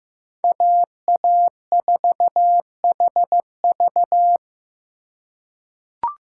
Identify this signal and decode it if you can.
Morse code, heard as a keyed tone, about 15 wpm, 700 Hz. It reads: AA4HV
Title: K1USN SST QSO Practice - 15wpm